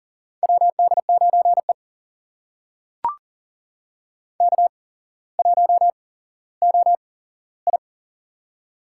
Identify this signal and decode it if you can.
Morse code, heard as a keyed tone, about 40 wpm, spaced as for 17 wpm, 700 Hz. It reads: WD9E K1OI
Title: USA Call Signs - 8x Character Spacing - 40wpm